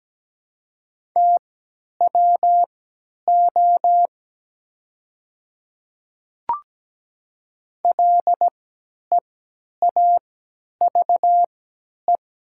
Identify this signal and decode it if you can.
Morse code, heard as a keyed tone, about 17 wpm, spaced as for 10 wpm, 700 Hz. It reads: TWO LEAVE
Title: 3x Character Spacing - Top 500 Words - 17wpm